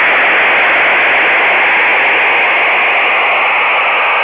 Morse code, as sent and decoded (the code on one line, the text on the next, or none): none